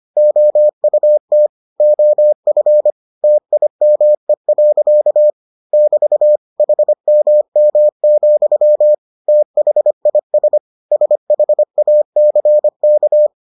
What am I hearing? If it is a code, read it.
Morse code, heard as a keyed tone, about 25 wpm, 600 Hz. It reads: OUT OF TIME. <BT> HMM, THIS SHACK